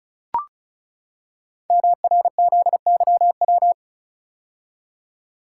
Morse code, heard as a keyed tone, about 36 wpm, 700 Hz. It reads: MRZYW
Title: Code Groups - 5 Letters - 35wpm